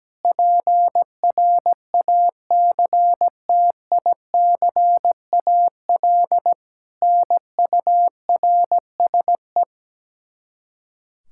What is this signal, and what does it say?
Morse code, heard as a keyed tone, about 17 wpm, 700 Hz. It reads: PRACTICAL NURSE
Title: Practice QSO - Exchange 3 - More Info - 17wpm